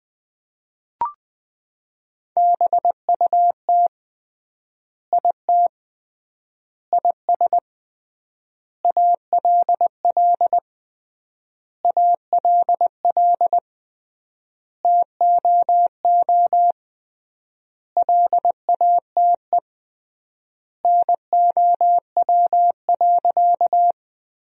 Morse code, heard as a keyed tone, about 20 wpm, 700 Hz. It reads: BUT IT IS ALL ALL TOO LATE NOW.
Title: 3x Word-Spacing - Sentences from Top 300 Words - 20wpm